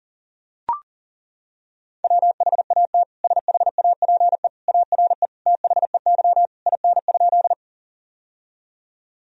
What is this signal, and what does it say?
Morse code, heard as a keyed tone, about 40 wpm, 700 Hz. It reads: WHAT SHAPE ARE THEY IN?
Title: Sentences from Top 1000 Words - Review Part 3 - 40wpm